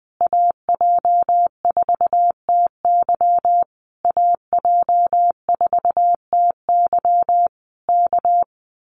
Morse code, heard as a keyed tone, about 20 wpm, 700 Hz. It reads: AJ4TY AJ4TY K
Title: Common QSO Phrases Encore 20wpm